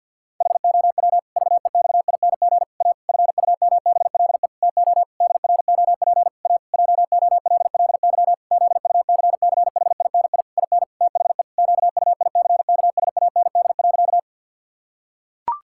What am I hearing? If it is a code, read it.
Morse code, heard as a keyed tone, about 51 wpm, 700 Hz. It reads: HOW VEXING A FUMBLE TO DROP A JOLLY ZUCCHINI IN THE QUICKSAND.
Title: Pangram Challenge v2 - 50wpm